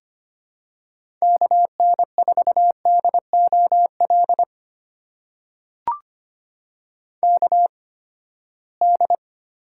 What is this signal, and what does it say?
Morse code, heard as a keyed tone, about 25 wpm, spaced as for 17 wpm, 700 Hz. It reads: KN4DOL KD
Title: USA Call Signs - 8x Character Spacing - 25wpm